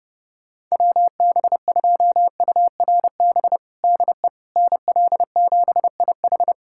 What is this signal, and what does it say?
Morse code, heard as a keyed tone, about 32 wpm, 700 Hz. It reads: WB2URB DE NL7IH
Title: USA Callsign DE Callsign 30wpm